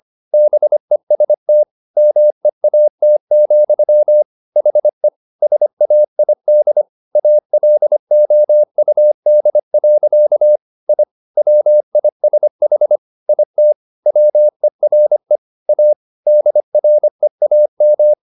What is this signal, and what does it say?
Morse code, heard as a keyed tone, about 25 wpm, 600 Hz. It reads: BEST MEAT, HE SAID ALOUD. I WISH IT WERE A DREAM